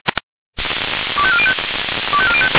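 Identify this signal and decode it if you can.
Five-tone selective call (ZVEI), heard as quick runs of five tones, tones of about 70 ms. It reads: 25405
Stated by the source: Received by SDR.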